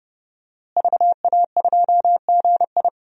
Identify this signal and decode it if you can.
Morse code, heard as a keyed tone, about 30 wpm, 700 Hz. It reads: VA2GI
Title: DXCC-100 Call Signs - 5 Characters - Repeated 2x - 30wpm